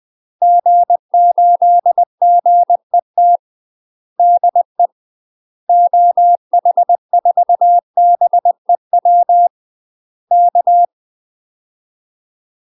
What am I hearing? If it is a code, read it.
Morse code, heard as a keyed tone, about 20 wpm, 700 Hz. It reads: G8GET DE OH4BEW K
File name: UK MASTER QSOs TRAINING / 20WPM / UK MASTER QSOs 0062